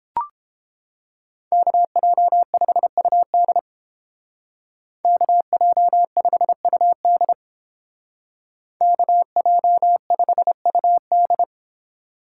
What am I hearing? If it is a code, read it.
Morse code, heard as a keyed tone, about 31 wpm, 700 Hz. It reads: KJ5UD KJ5UD KJ5UD